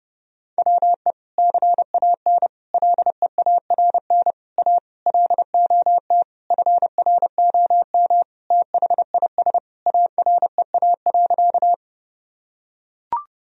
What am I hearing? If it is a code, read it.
Morse code, heard as a keyed tone, about 30 wpm, 700 Hz. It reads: WE CAN LEARN A LOT FROM THIS AREA.
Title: Sentences from Top 1000 Words - Review Part 2 - 30wpm